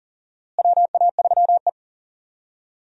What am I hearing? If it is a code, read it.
Morse code, heard as a keyed tone, about 40 wpm, 700 Hz. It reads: WA3E
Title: USA Call Signs - 4 Characters - Repeated 2x - 40wpm